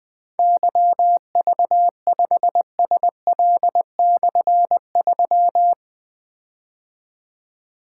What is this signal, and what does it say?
Morse code, heard as a keyed tone, about 20 wpm, 700 Hz. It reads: YV5SL/3
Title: Difficult Call Signs 20wpm